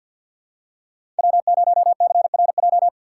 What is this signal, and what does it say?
Morse code, heard as a keyed tone, about 52 wpm, 700 Hz. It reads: W0CRJ
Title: USA Call Signs - 4x Character Spacing - 50wpm